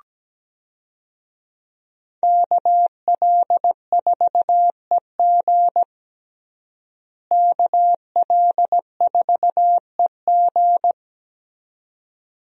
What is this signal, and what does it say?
Morse code, heard as a keyed tone, about 17 wpm, 700 Hz. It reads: KL4EG KL4EG
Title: USA Call Signs - 5 Characters - Repeated 2x - 17wpm